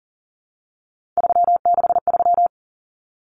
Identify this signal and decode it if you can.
Morse code, heard as a keyed tone, about 40 wpm, 700 Hz. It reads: 363
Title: Common Cut Numbers - 3-Digit 40wpm